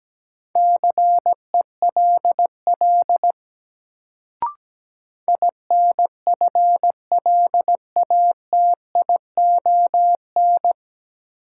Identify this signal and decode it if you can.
Morse code, heard as a keyed tone, about 17 wpm, 700 Hz. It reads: CELL INFLATION
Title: Sentences from Top 1100 Words - New Word Drill - 17wpm